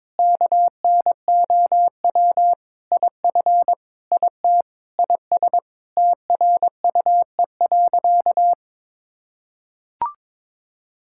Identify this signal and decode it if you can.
Morse code, heard as a keyed tone, about 22 wpm, 700 Hz. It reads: KNOW IF IT IS TRUE.
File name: practice / 22wpm / Sentences with 4-Word Phrases - 22wpm